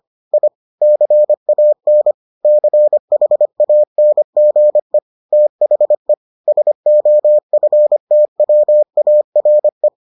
Morse code, heard as a keyed tone, about 25 wpm, 600 Hz. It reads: I CAN CHANGE THE SOFTWARE